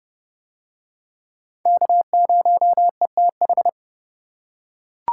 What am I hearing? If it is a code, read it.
Morse code, heard as a keyed tone, about 27 wpm, 700 Hz. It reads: K0ETH